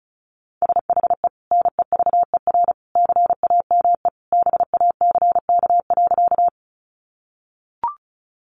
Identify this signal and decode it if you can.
Morse code, heard as a keyed tone, about 34 wpm, 700 Hz. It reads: SHE NEVER CAME BACK.
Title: Sentences from Top 300 Words 35wpm